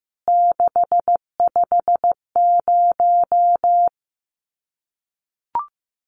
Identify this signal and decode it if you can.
Morse code, heard as a keyed tone, about 15 wpm, 700 Hz. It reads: 650
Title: CWT Exchange 15wpm